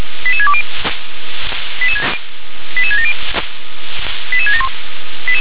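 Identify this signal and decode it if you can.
Five-tone selective call (ZVEI), heard as quick runs of five tones, tones of about 70 ms. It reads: 88620 88670 88671
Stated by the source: Received by SDR, ZVEI five-tone.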